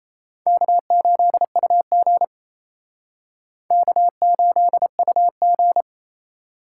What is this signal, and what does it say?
Morse code, heard as a keyed tone, about 36 wpm, 700 Hz. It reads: K8UG K8UG